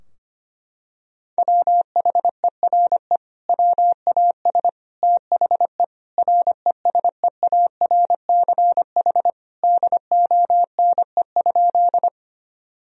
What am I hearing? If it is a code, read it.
Morse code, heard as a keyed tone, about 25 wpm, 700 Hz. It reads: WHERE WAS THE RESEARCH DONE?
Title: Sentences from Top 1000 Words - Review Part 4 - 25wpm